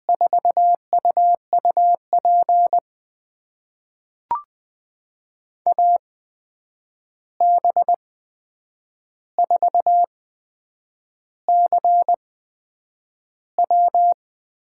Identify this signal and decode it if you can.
Morse code, heard as a keyed tone, about 20 wpm, spaced as for 8 wpm, 700 Hz. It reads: 4UUP AB4CW
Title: USA Call Signs - 8x Character Spacing - 20wpm